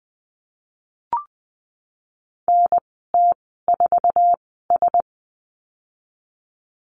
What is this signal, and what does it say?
Morse code, heard as a keyed tone, about 20 wpm, spaced as for 15 wpm, 700 Hz. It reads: NT4S
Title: 055 Lesson 10g - Letter r - Call Signs 20wpm